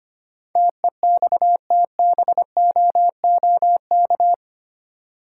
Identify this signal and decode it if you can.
Morse code, heard as a keyed tone, about 28 wpm, 700 Hz. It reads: TEXTBOOK
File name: Easy Words - Speed-Racing with Hints - 25wpm